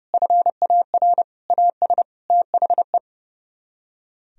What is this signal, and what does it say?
Morse code, heard as a keyed tone, about 35 wpm, 700 Hz. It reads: FAR AS THE
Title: Speed-Racing - Sets of 4 Words 30wpm